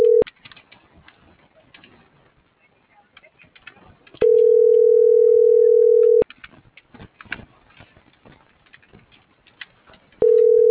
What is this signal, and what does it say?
Ringback tone (North America), 440 + 480 Hz.